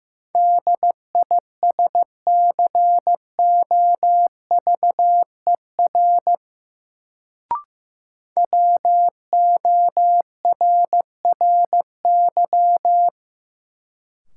Morse code, heard as a keyed tone, about 15 wpm, 700 Hz. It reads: DISCOVER WORRY
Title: Sentences from Top 1100 Words - New Word Drill Alt - 15wpm